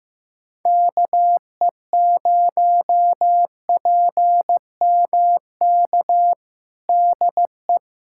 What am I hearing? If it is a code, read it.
Morse code, heard as a keyed tone, about 15 wpm, 700 Hz. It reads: KE0PMK DE
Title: USA Callsign DE Callsign - 6 Characters - 15wpm